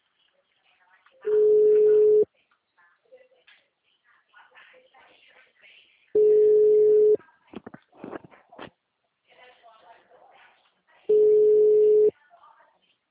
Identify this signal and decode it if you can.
Ringback tone (Europe, 425 Hz), with speech.
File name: out- -2025- - - .899